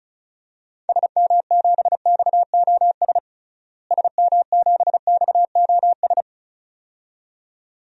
Morse code, heard as a keyed tone, about 36 wpm, 700 Hz. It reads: SM7XOS SM7XOS
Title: DXCC-100 Call Signs - 6 Characters - Repeated 2x - 35wpm